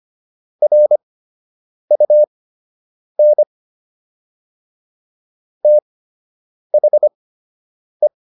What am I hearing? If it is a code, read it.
Morse code, heard as a keyed tone, about 25 wpm, spaced as for 8 wpm, 600 Hz. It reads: RUN THE